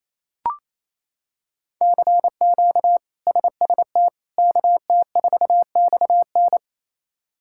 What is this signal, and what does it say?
Morse code, heard as a keyed tone, about 28 wpm, 700 Hz. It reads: CQ SST KT4XN
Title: K1USN SST QSO Practice - 28wpm